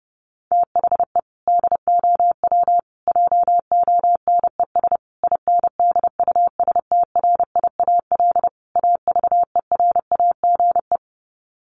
Morse code, heard as a keyed tone, about 30 wpm, 700 Hz. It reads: THE DOW JONES INDUSTRIAL AVERAGE
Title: Sets of 5 Words 30wpm